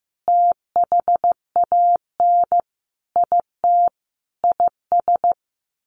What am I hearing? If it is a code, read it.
Morse code, heard as a keyed tone, about 15 wpm, 700 Hz. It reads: THAN IT IS
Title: Sentences from Top 1200 Words 15wpm